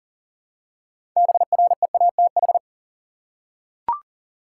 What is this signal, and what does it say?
Morse code, heard as a keyed tone, about 40 wpm, 700 Hz. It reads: BREATH
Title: Six Letter Words - 40wpm